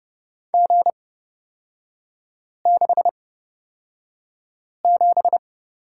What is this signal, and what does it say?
Morse code, heard as a keyed tone, about 30 wpm, spaced as for 5 wpm, 700 Hz. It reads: G67
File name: Lesson 37e - Letter x - Three Characters 30wpm